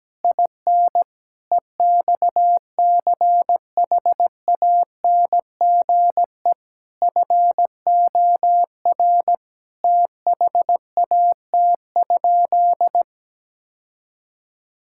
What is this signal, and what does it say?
Morse code, heard as a keyed tone, about 17 wpm, 700 Hz. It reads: IN EXCHANGE FOR THAT?
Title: Sentences from Top 1000 Words - Review Part 6 - 17wpm